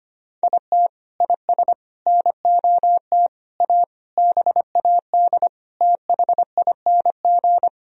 Morse code, heard as a keyed tone, about 25 wpm, 700 Hz. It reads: IT IS NOT A BAD THING
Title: Sentences from Top 1000 Words - Review Part 4 - 25wpm